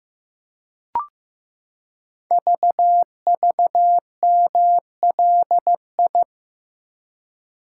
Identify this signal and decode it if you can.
Morse code, heard as a keyed tone, about 15 wpm, 700 Hz. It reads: VVMLI
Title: Code Groups - 5 Letters - 15wpm